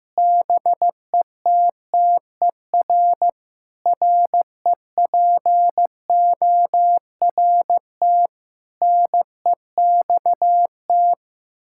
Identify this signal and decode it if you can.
Morse code, heard as a keyed tone, about 15 wpm, 700 Hz. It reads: BETTER REPORT NEXT
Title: Sentences from Top 1000 Words - Review Part 6 - 15wpm